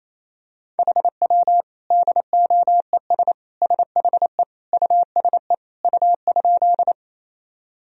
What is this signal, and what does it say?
Morse code, heard as a keyed tone, about 28 wpm, 700 Hz. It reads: HW DOES SHE USE U?
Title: Sentences from Top 200 Words and Abbreviations 28wpm